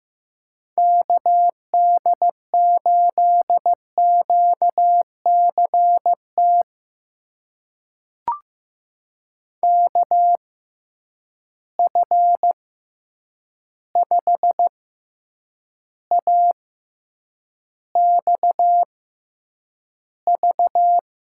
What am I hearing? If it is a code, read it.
Morse code, heard as a keyed tone, about 15 wpm, spaced as for 8 wpm, 700 Hz. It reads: KD8QCT KF5AXV
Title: USA Call Signs - 6 Characters - 6x Character Spacing - 15wpm